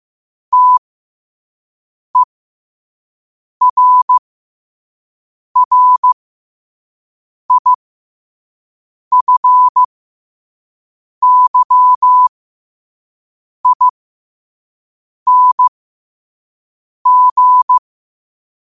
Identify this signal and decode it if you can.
Morse code, heard as a keyed tone, about 15 wpm, spaced as for 5 wpm, 1000 Hz. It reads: TERRIFYING
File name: qotdfiles / 5wpm